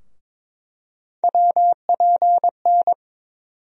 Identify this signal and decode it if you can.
Morse code, heard as a keyed tone, about 22 wpm, 700 Hz. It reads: WPN
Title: USA Call Sign Postfixes - 3 Letters 22wpm